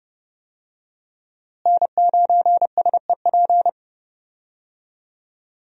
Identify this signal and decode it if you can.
Morse code, heard as a keyed tone, about 32 wpm, 700 Hz. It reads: N9SEP
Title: USA Call Signs - 5 Characters - Repeated 2x - 30wpm